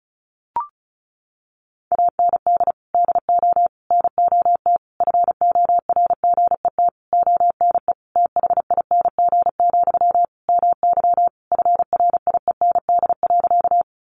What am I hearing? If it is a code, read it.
Morse code, heard as a keyed tone, about 34 wpm, 700 Hz. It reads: AND DO NOT FORGET ONE THING, MY FRIEND.
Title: Sentences from Top 1200 Words 35wpm